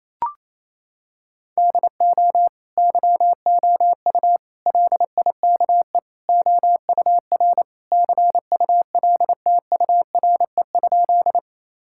Morse code, heard as a keyed tone, about 28 wpm, 700 Hz. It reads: DO YOU LIKE OUR CULTURE?
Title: Sentences from Top 1000 Words - Review Part 6 - 28wpm